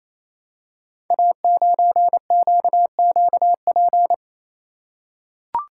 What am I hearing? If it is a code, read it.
Morse code, heard as a keyed tone, about 28 wpm, 700 Hz. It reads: A9QQP
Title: Code Groups - 5 Letter:Numbers - 28wpm